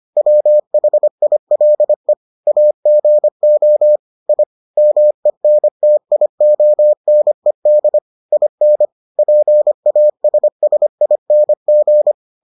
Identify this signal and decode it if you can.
Morse code, heard as a keyed tone, about 25 wpm, 600 Hz. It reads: WHILE AGO I MENTIONED IN PASSING